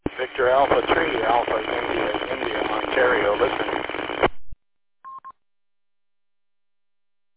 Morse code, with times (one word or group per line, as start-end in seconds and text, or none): none